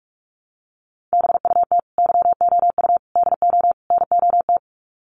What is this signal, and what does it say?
Morse code, heard as a keyed tone, about 44 wpm, 700 Hz. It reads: BUT YOU DO NOT
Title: Sets of 4 Words 45wpm